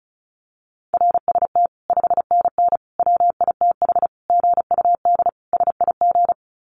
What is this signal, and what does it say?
Morse code, heard as a keyed tone, about 34 wpm, 700 Hz. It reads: RST 5NN WITH GUD SIG
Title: Common QSO Phrases 35wpm